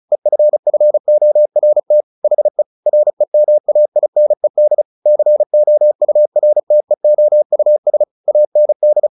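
Morse code, heard as a keyed tone, about 34 wpm, 600 Hz. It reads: EFFORT HE REMAINED COURTEOUS AND
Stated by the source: Morse practice at 35 wpm from CW med Gnister, sending Ken Follett's novel Never in English.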